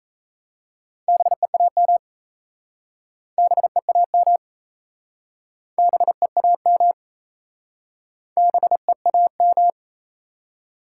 Morse code, heard as a keyed tone, about 39 wpm, 700 Hz. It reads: BEAM BEAM BEAM BEAM